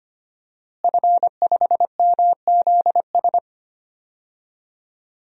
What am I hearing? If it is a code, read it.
Morse code, heard as a keyed tone, about 25 wpm, 700 Hz. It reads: F5MZS
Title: DXCC-100 Call Signs - 5 Characters - Repeated 3x - 25wpm